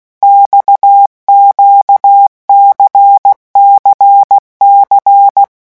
Morse code, heard as a keyed tone, about 16 wpm, 800 Hz. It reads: XQCCC
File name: morse mp3 / revision2-vitesse 15 mots